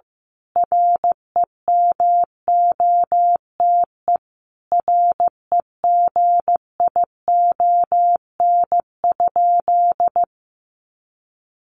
Morse code, heard as a keyed tone, about 15 wpm, 700 Hz. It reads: REMOTE REGION?
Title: Sentences from Top 1500 Words 15wpm